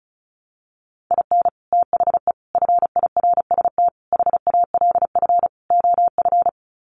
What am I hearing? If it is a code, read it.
Morse code, heard as a keyed tone, about 34 wpm, 700 Hz. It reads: IN THE FIRST HALF OF
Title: Sets of 5 Words 35wpm